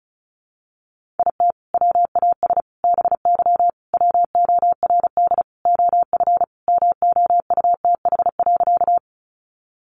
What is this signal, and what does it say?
Morse code, heard as a keyed tone, about 34 wpm, 700 Hz. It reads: IT WAS BY WORD OF MOUTH.
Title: Sentences from Top 1500 Words 35wpm